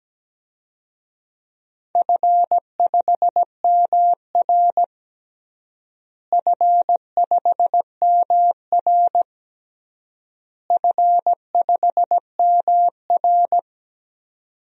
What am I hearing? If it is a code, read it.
Morse code, heard as a keyed tone, about 17 wpm, 700 Hz. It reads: F5MR F5MR F5MR
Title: DXCC-100 Call Signs - 4 Characters - Repeated 3x - 17wpm